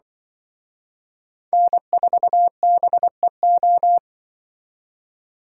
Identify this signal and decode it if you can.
Morse code, heard as a keyed tone, about 25 wpm, 700 Hz. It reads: N4BEO
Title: Speed Racing - USA 1x3 Call Signs 20wpm